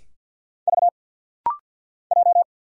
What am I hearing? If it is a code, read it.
Morse code, heard as a keyed tone, about 50 wpm, 700 Hz. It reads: V J
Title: Single Letters - Rapid-Fire 50wpm